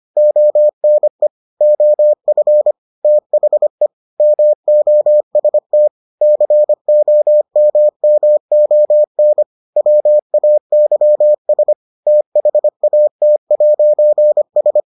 Morse code, heard as a keyed tone, about 25 wpm, 600 Hz. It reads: ONE OF THE MOST COMMON WAYS THAT'S